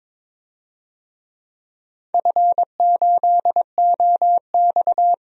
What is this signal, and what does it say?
Morse code, heard as a keyed tone, about 22 wpm, 700 Hz. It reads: F8OX
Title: DXCC-100 Call Signs - 4 Characters - Repeated 2x - 22wpm